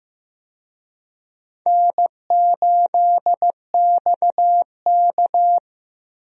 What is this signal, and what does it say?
Morse code, heard as a keyed tone, about 15 wpm, 700 Hz. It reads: N8XK
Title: USA Call Signs - 4 Characters - Repeated 2x - 15wpm